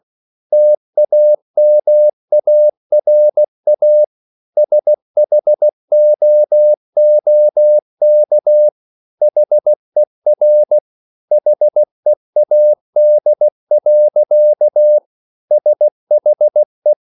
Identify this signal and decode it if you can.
Morse code, heard as a keyed tone, about 16 wpm, 600 Hz. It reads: TAMARA SHOOK HER HEAD. SHE